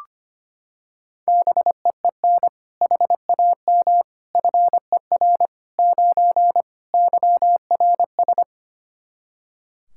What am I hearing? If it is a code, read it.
Morse code, heard as a keyed tone, about 25 wpm, 700 Hz. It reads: BEEN HAM FER 9 YRS
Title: Practice QSO - Exchange 3 - More Info - 25wpm